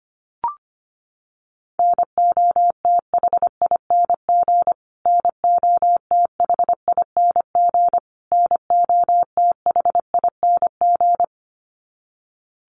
Morse code, heard as a keyed tone, about 25 wpm, 700 Hz. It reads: NOTHING NOTHING NOTHING
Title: Top 500 Words Repeated 25wpm